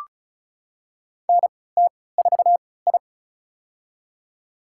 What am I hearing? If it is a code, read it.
Morse code, heard as a keyed tone, about 36 wpm, spaced as for 20 wpm, 700 Hz. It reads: NT4I
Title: USA Call Signs - 4 Characters - 3x Character Spacing - 35wpm